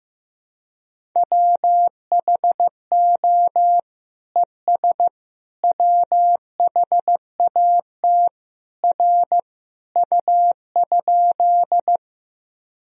Morse code, heard as a keyed tone, about 15 wpm, 700 Hz. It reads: WHO ES WHAT R U?
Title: Sentences from Top 100 Words and Abbreviations 15wpm